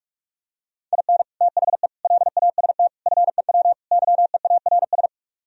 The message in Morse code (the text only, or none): IN THE LAST FEW YEARS